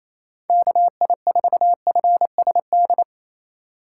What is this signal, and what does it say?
Morse code, heard as a keyed tone, about 27 wpm, 700 Hz. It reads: KI4FSD
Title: USA Call Signs - 6 Characters - 3x Character Spacing - 28wpm